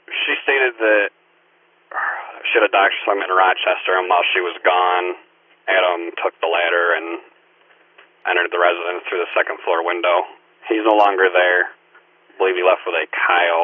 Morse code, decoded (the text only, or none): none